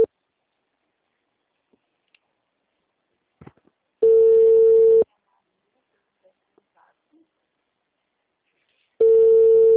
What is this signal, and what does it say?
Ringback tone (China, 450 Hz).
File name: out- -2009- - - .108